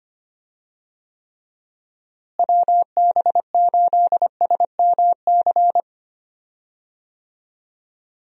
Morse code, heard as a keyed tone, about 25 wpm, 700 Hz. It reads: WB8SMC
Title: USA Call Signs - 6 Characters - Repeated 2x - 25wpm